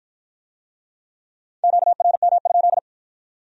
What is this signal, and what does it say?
Morse code, heard as a keyed tone, about 51 wpm, 700 Hz. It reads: QRM?